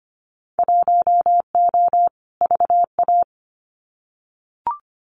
Morse code, heard as a keyed tone, about 25 wpm, 700 Hz. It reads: 1O VA
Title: Winter Field Day Exchanges 25wpm